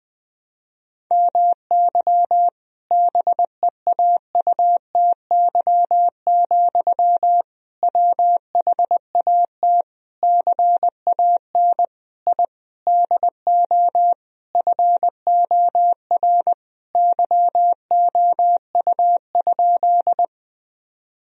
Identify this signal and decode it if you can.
Morse code, heard as a keyed tone, about 21 wpm, 700 Hz. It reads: MY BEAUTY, WHAT CAN I DO FOR YOU?
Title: Sentences from Top 1000 Words - Review Part 2 - 20wpm